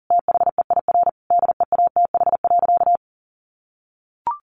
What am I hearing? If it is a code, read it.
Morse code, heard as a keyed tone, about 40 wpm, 700 Hz. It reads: THEIR DEATH.
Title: Sentences from Top 1100 Words 40wpm